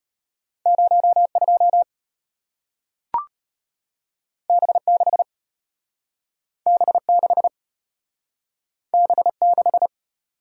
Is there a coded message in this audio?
Yes: Morse code from 0.7 to 9.9 s, heard as a keyed tone, about 33 wpm, 700 Hz. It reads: 02 B6 B6 B6